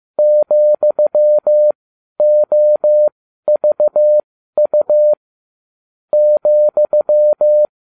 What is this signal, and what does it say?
Morse code, heard as a keyed tone, about 15 wpm, spaced as for 12 wpm, 600 Hz. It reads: ,OVU ,